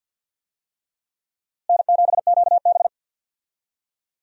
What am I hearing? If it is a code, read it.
Morse code, heard as a keyed tone, about 46 wpm, 700 Hz. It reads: N7QB
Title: USA Call Signs - 4 Characters - Repeated 3x - 50wpm